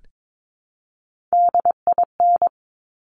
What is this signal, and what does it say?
Morse code, heard as a keyed tone, about 22 wpm, 700 Hz. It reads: DIN
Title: Top Three Letter Combinations in Words 22wpm